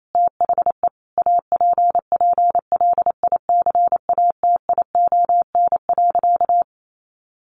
Morse code, heard as a keyed tone, about 28 wpm, 700 Hz. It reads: THE APPLICATION.
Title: Sentences from Top 1200 Words 28wpm